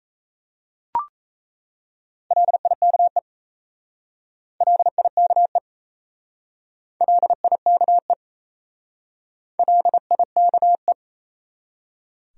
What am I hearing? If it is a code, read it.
Morse code, heard as a keyed tone, about 34 wpm, 700 Hz. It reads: LIKE LIKE LIKE LIKE
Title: Speed-Racing Top 100 Words 28wpm